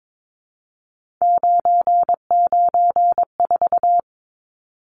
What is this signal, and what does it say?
Morse code, heard as a keyed tone, about 22 wpm, 700 Hz. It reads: 994
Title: Numbers - 3-Digit 22wpm